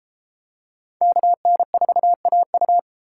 Morse code, heard as a keyed tone, about 30 wpm, 700 Hz. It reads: KN4AU